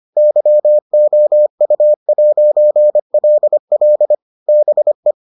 Morse code, heard as a keyed tone, about 25 wpm, 600 Hz. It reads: YOU'LL BE